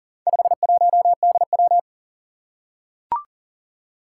Morse code, heard as a keyed tone, about 42 wpm, 700 Hz. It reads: 51DW
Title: Four Challenging Letter-Number Combinations 40wpm